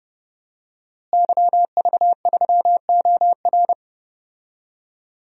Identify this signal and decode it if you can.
Morse code, heard as a keyed tone, about 30 wpm, 700 Hz. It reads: YV3OR
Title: DXCC-100 Call Signs - 5 Characters - Repeated 3x - 30wpm